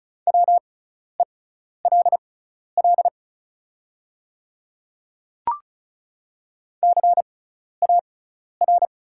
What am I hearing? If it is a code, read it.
Morse code, heard as a keyed tone, about 36 wpm, spaced as for 12 wpm, 690 Hz. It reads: WELL CAR
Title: 6x Character Spacing - Top 300 Words - 35wpm